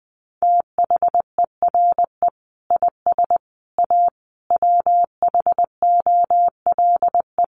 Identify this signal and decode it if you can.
Morse code, heard as a keyed tone, about 20 wpm, 700 Hz. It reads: THERE IS A WHOLE
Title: Sentences from Top 1400 Words 20wpm